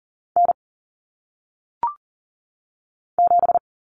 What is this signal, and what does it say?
Morse code, heard as a keyed tone, about 40 wpm, 700 Hz. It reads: N 7
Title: Common Cut Numbers - 1-Digit 40wpm